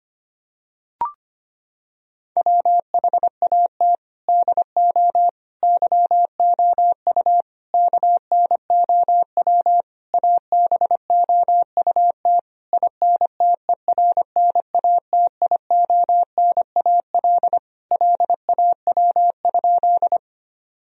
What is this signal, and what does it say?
Morse code, heard as a keyed tone, about 25 wpm, 700 Hz. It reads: WHAT DO YOU KNOW ABOUT INTERNATIONAL LAW?
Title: Sentences from Top 1000 Words - Review Part 4 - 25wpm